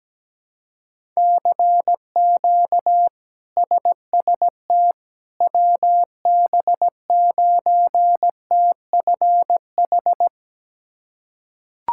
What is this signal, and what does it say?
Morse code, heard as a keyed tone, about 17 wpm, 700 Hz. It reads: CQ SST WB9TFH
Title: K1USN SST QSO Practice - 17wpm